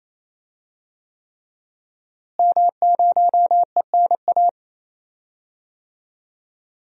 Morse code, heard as a keyed tone, about 28 wpm, 700 Hz. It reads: M0ENA
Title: DXCC-100 Call Signs - 5 Characters - Repeated 3x - 28wpm